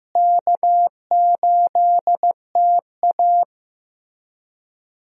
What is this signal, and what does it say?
Morse code, heard as a keyed tone, about 15 wpm, 700 Hz. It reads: K8TA
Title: USA Call Signs - 4 Characters - Repeated 2x - 15wpm